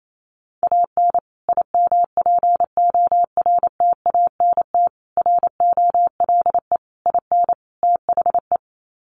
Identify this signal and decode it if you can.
Morse code, heard as a keyed tone, about 28 wpm, 700 Hz. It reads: AN IMPORTANT ROLE IN THE
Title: Sets of 5 Words Encore2 28wpm